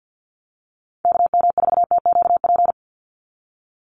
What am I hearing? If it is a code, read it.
Morse code, heard as a keyed tone, about 50 wpm, 700 Hz. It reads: KM4TQP
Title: USA Call Signs 50wpm